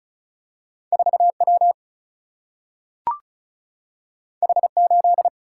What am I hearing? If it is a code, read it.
Morse code, heard as a keyed tone, about 36 wpm, 700 Hz. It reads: 4 W H 8
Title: Single-Letter Single-Number 35wpm